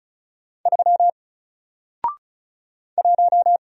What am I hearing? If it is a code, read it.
Morse code, heard as a keyed tone, about 36 wpm, 700 Hz. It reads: 3 1
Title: Single Letter-Number Alt 35wpm